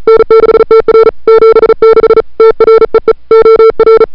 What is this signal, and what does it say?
Morse code, heard as a keyed tone, about 36 wpm, 1340 Hz, with noise. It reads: N6TR 76 TREE OR
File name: n6tr / Fab2007CWSprintHour1